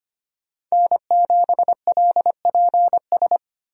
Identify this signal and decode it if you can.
Morse code, heard as a keyed tone, about 25 wpm, 700 Hz. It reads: N7LPS
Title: USA Call Signs - 5 Characters - Repeated 3x - 25wpm